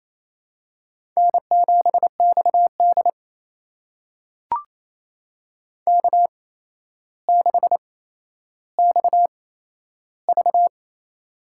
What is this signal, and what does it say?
Morse code, heard as a keyed tone, about 28 wpm, spaced as for 12 wpm, 700 Hz. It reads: N7XD K6XV
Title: USA Call Signs - 4 Characters - 8x Character Spacing - 28wpm